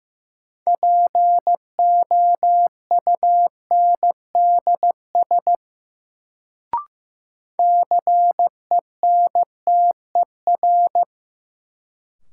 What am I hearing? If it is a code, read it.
Morse code, heard as a keyed tone, about 15 wpm, 700 Hz. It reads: POUNDS CENTER
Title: Sentences from Top 1000 Words - Review Part 2 - New Word Drill - 15wpm